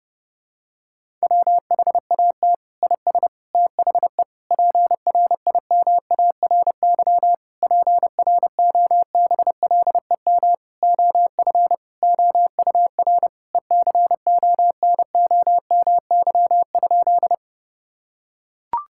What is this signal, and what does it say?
Morse code, heard as a keyed tone, about 30 wpm, 700 Hz. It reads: WHAT IS THE PRIMARY PROBLEM OF OUR ECONOMY?
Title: Sentences from Top 1000 Words - Review Part 5 - 30wpm